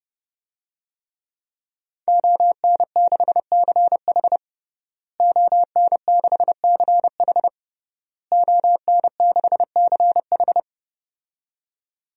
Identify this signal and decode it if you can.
Morse code, heard as a keyed tone, about 32 wpm, 700 Hz. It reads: ON6CH ON6CH ON6CH
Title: DXCC-100 Call Signs - 5 Characters - Repeated 3x - 30wpm